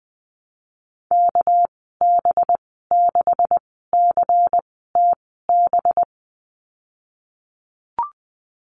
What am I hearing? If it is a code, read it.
Morse code, heard as a keyed tone, about 20 wpm, spaced as for 15 wpm, 700 Hz. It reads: KB6CTB
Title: 207 Lesson 33g - Letter j - Call Signs 20wpm